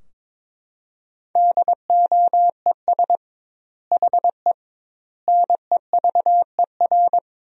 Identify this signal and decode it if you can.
Morse code, heard as a keyed tone, about 22 wpm, 700 Hz. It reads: DOES HE NEVER
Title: 2x Word-Spacing - Sentences from Top 300 Words - 22wpm